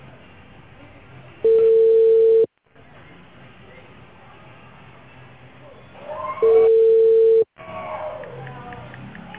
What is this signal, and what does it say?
Ringback tone (China, 450 Hz).